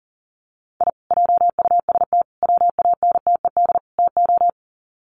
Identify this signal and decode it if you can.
Morse code, heard as a keyed tone, about 40 wpm, 700 Hz. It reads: I JUST WANTED TO
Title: Sets of 4 Words 40wpm